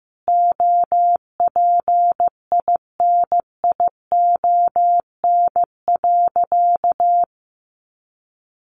Morse code, heard as a keyed tone, about 15 wpm, 700 Hz. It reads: OPINION.
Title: Sentences from Top 1200 Words 15wpm